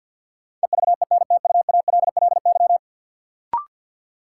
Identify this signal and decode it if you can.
Morse code, heard as a keyed tone, about 52 wpm, 700 Hz. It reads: EVENTUALLY
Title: Sentences from Top 1000 Words - Review Part 6 - New Word Drill - 50wpm